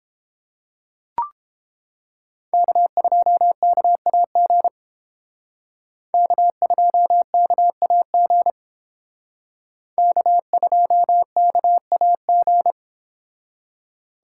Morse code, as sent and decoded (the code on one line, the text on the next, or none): -.- ..--- -.- .- --. / -.- ..--- -.- .- --. / -.- ..--- -.- .- --.
K2KAG K2KAG K2KAG